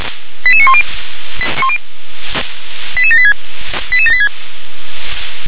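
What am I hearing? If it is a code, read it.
Five-tone selective call (ZVEI), heard as quick runs of five tones, tones of about 70 ms. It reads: 88910 88786 88786
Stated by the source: ZVEI five-tone selective call 88983 received on an SDR.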